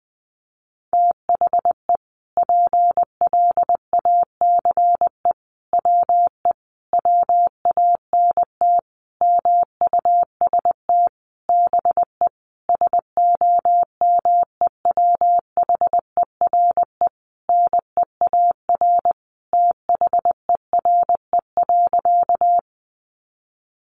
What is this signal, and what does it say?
Morse code, heard as a keyed tone, about 20 wpm, 700 Hz. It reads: THE PLACE WE WANT MUST BE SOMEWHERE NEAR THERE.
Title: Sentences from Top 1400 Words 20wpm